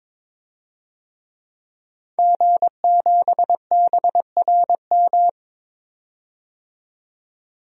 Morse code, heard as a keyed tone, about 22 wpm, 700 Hz. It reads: G7BRM
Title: DXCC-100 Call Signs - 5 Characters - 22wpm